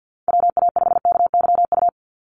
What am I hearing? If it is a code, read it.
Morse code, heard as a keyed tone, about 48 wpm, 700 Hz. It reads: WA5KYU
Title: USA Call Signs 50wpm